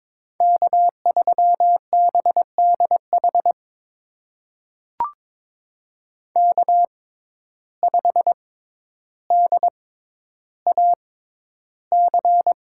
Morse code, heard as a keyed tone, about 22 wpm, spaced as for 11 wpm, 700 Hz. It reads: K3BDH K5DAC
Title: USA Call Signs - 5 Characters - 6x Character Spacing - 22wpm